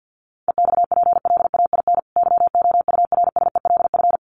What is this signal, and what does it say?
Morse code, heard as a keyed tone, about 50 wpm, 700 Hz. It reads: EXPLAIN YOURSELF
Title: Sentences from Top 1300 Words 50wpm